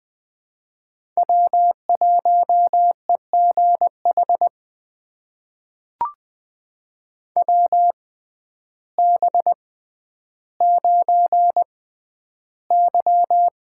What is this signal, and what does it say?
Morse code, heard as a keyed tone, about 20 wpm, spaced as for 11 wpm, 700 Hz. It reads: W1EGH WB9Y
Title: USA Call Signs - 6x Character Spacing - 20wpm